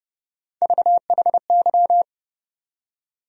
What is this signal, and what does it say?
Morse code, heard as a keyed tone, about 30 wpm, 700 Hz. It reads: VHY
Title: USA Call Sign Postfixes - 3 Letters 30wpm